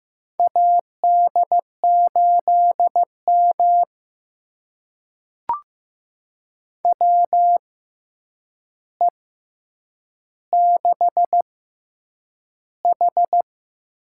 Morse code, heard as a keyed tone, about 15 wpm, spaced as for 8 wpm, 700 Hz. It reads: AD8M WE6H
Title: USA Call Signs - 4 Characters - 6x Character Spacing - 15wpm